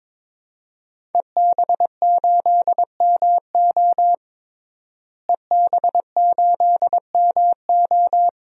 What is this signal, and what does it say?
Morse code, heard as a keyed tone, about 23 wpm, 700 Hz. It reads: EB8MO EB8MO
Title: DXCC-100 Call Signs - 5 Characters - Repeated 2x - 22wpm